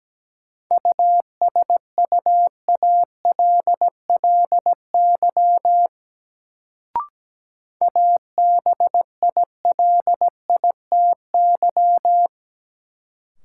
Morse code, heard as a keyed tone, about 17 wpm, 700 Hz. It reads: USUALLY ABILITY
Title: Sentences from Top 1000 Words - Review Part 3 - New Word Drill - 17wpm